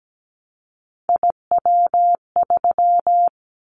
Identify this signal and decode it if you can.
Morse code, heard as a keyed tone, about 17 wpm, 700 Hz. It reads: IW3
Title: Callsign Prefixes for Top 25 DXCC - 17wpm